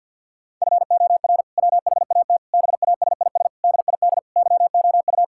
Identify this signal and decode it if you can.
Morse code, heard as a keyed tone, about 50 wpm, 700 Hz. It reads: FOR WHAT BASIS DID YOU
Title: Sentences from Top 1000 Words - Review Part 5 - 50wpm